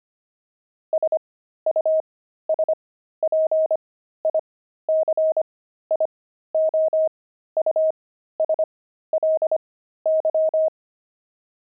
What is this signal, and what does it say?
Morse code, heard as a keyed tone, about 25 wpm, spaced as for 13 wpm, 630 Hz. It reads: SUSPICIOUSLY
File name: pangram-fs-13-cs-25-squirrel